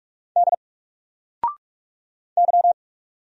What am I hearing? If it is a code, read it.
Morse code, heard as a keyed tone, about 45 wpm, 700 Hz. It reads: D Y
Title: Single Letter-Number Alt 45wpm